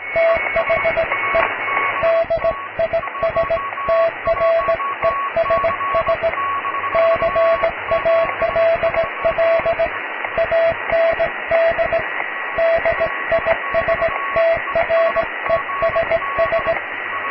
Morse code, heard as a keyed tone, about 18 wpm, 660 Hz, with radio noise. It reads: THE DISTRESS CALL AND DISTRESS
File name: cw / tekst(b)+QRM